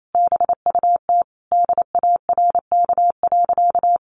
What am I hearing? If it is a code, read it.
Morse code, heard as a keyed tone, about 28 wpm, 700 Hz. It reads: BUT DARK.
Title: Sentences from Top 1400 Words 28wpm